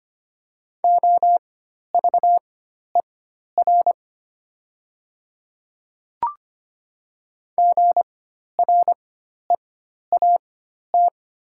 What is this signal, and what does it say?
Morse code, heard as a keyed tone, about 25 wpm, spaced as for 12 wpm, 700 Hz. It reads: OVER GREAT
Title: 4x Character Spacing - Top 200 Words - 25wpm